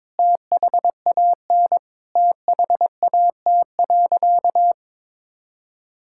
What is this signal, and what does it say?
Morse code, heard as a keyed tone, about 22 wpm, 700 Hz. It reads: THAN THAT.
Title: Sentences from Top 1600 Words 22wpm